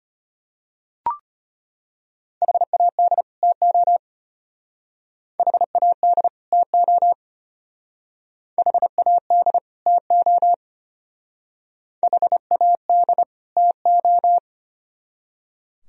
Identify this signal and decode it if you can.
Morse code, heard as a keyed tone, about 33 wpm, 700 Hz. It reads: HAD TO HAD TO HAD TO HAD TO